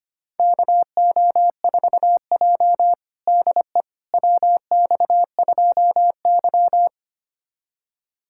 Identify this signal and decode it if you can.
Morse code, heard as a keyed tone, about 25 wpm, 700 Hz. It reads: KO4J DE WX2Y
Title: USA Callsign DE Callsign - 4 Characters - 25wpm